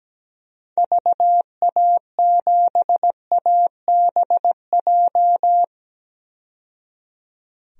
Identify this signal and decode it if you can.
Morse code, heard as a keyed tone, about 17 wpm, 700 Hz. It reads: VA7ABJ
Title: DXCC-100 Call Signs - 6 Characters - 17wpm